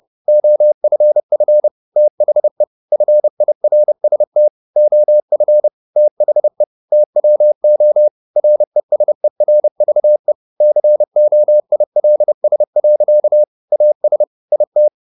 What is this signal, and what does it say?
Morse code, heard as a keyed tone, about 30 wpm, 600 Hz. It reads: OFF THE FIRST OF THE TWO RESERVE COILS. AS IT
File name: the old man and the sea / the old man and the sea 1 30wpm